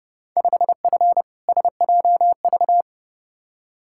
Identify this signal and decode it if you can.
Morse code, heard as a keyed tone, about 30 wpm, 700 Hz. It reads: 5F SJV
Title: Field Day Exchanges Encore - 30wpm